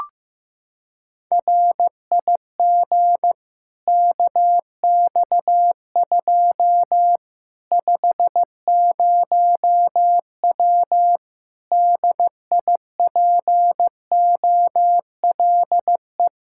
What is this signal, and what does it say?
Morse code, heard as a keyed tone, about 15 wpm, 700 Hz. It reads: RIG KX2 50W DIPOLE
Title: K1USN SST QSO Practice 15wpm